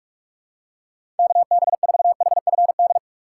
Morse code, heard as a keyed tone, about 45 wpm, 700 Hz. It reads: KB4HFD
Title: USA Call Signs - 4x Character Spacing - 45wpm